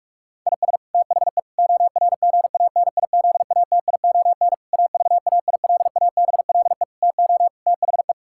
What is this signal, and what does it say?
Morse code, heard as a keyed tone, about 46 wpm, 690 Hz. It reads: IS THE ORGANIZATION AVAILABLE TO THE
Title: Sentences from Top 1000 Words - Review Part 5 - 45wpm